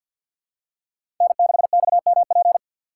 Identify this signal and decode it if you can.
Morse code, heard as a keyed tone, about 51 wpm, 700 Hz. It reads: N6XMP
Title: USA Call Signs - 3x Character Spacing - 50wpm